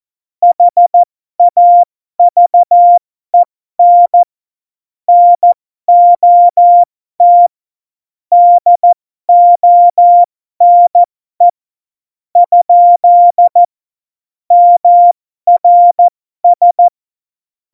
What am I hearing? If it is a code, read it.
Morse code, heard as a keyed tone, about 14 wpm, spaced as for 12 wpm, 700 Hz. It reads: HAVEN NOT DONE ? MRS
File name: Morsekurs / Witze